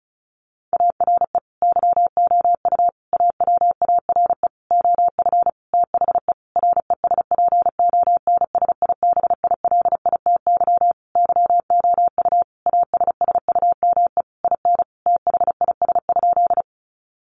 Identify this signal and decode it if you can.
Morse code, heard as a keyed tone, about 34 wpm, 700 Hz. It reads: ARE YOU AWARE OF THE RESPONSIBILITY YOU ASSUME IN THIS?
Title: Sentences from Top 1500 Words 35wpm